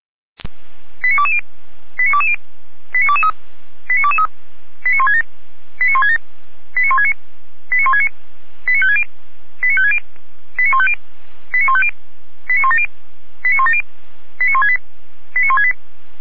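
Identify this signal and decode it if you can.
Five-tone selective call (ZVEI), heard as quick runs of five tones, tones of about 70 ms. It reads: 89220 89220 89223 89223 89167 89167 89169 89169 89560 89560 89150 89150 89170 89170 89168 89168
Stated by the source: ZVEI five-tone selective call 89310 received by SDR.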